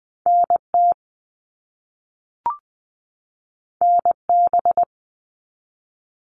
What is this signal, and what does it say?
Morse code, heard as a keyed tone, about 20 wpm, 700 Hz. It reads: NT NB
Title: Canadian Provinces and Territories 20wpm